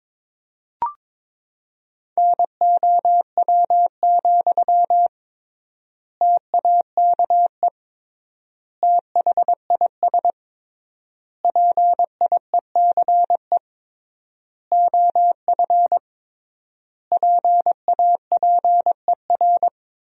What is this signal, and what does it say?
Morse code, heard as a keyed tone, about 22 wpm, 700 Hz. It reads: NOW, TAKE THIS PIECE OF PAPER
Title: 3x Word-Spacing - Sentences from Top 400 Words - 22wpm